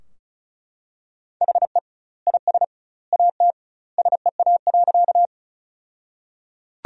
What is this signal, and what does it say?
Morse code, heard as a keyed tone, about 34 wpm, 700 Hz. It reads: HE IS AT SEA.
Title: 2x Word-Spacing - Sentences from Top 300 Words - 35wpm